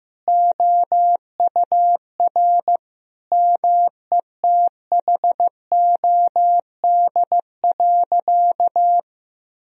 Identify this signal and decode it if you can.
Morse code, heard as a keyed tone, about 15 wpm, 700 Hz. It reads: OUR METHOD.
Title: Sentences from Top 1000 Words - Review Part 3 - 15wpm